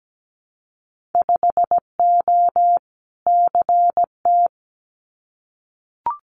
Morse code, heard as a keyed tone, about 17 wpm, 700 Hz. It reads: 5O CT
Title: Winter Field Day Exchanges Encore 17wpm